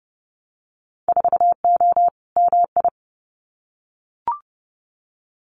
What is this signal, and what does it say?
Morse code, heard as a keyed tone, about 30 wpm, 700 Hz. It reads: 4O MI
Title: Winter Field Day Exchanges Encore 30wpm